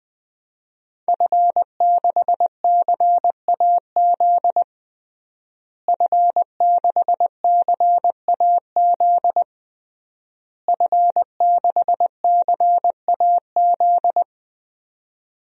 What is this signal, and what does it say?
Morse code, heard as a keyed tone, about 21 wpm, 700 Hz. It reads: F6CAZ F6CAZ F6CAZ
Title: DXCC-100 Call Signs - 5 Characters - Repeated 3x - 20wpm